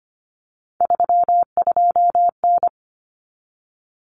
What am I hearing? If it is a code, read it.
Morse code, heard as a keyed tone, about 25 wpm, 700 Hz. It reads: 32N
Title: Common Cut Numbers - 3-Digit 25wpm